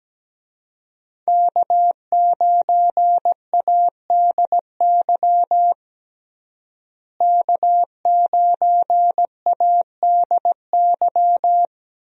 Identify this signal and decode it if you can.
Morse code, heard as a keyed tone, about 17 wpm, 700 Hz. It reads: K9ADY K9ADY
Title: USA Call Signs - 5 Characters - Repeated 3x - 17wpm